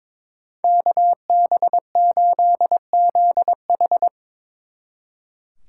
Morse code, heard as a keyed tone, about 24 wpm, 700 Hz. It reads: KB8ZH
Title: Speed Racing - USA Call Signs 22wpm